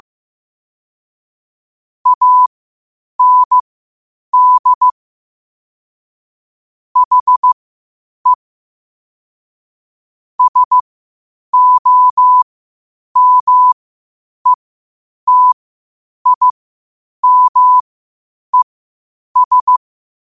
Morse code, heard as a keyed tone, about 15 wpm, spaced as for 8 wpm, 1000 Hz. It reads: AND HE SOMETIMES